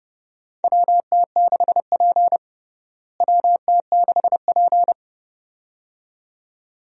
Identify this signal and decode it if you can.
Morse code, heard as a keyed tone, about 32 wpm, 700 Hz. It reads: WT6P WT6P
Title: USA Call Signs - 4 Characters - Repeated 2x - 30wpm